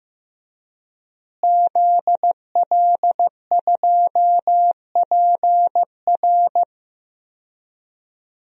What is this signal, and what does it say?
Morse code, heard as a keyed tone, about 15 wpm, 700 Hz. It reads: ZL2PR
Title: DXCC-100 Call Signs - 5 Characters - Repeated 3x - 15wpm